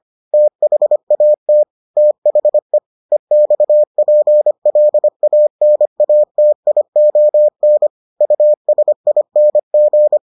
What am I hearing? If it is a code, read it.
Morse code, heard as a keyed tone, about 25 wpm, 600 Hz. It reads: THAT THE EXPLANATION USING